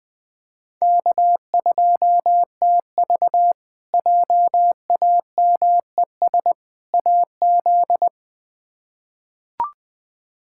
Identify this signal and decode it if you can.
Morse code, heard as a keyed tone, about 21 wpm, 700 Hz. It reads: K2TV JAMES AZ
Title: K1USN SST QSO Practice - 20wpm